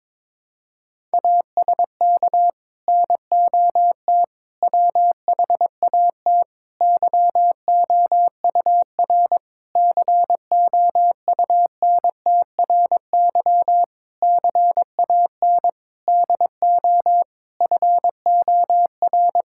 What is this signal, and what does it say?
Morse code, heard as a keyed tone, about 22 wpm, 700 Hz. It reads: ASK NOT WHAT YOUR COUNTRY CAN DO FOR
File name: Sets of 9 Words - 22wpm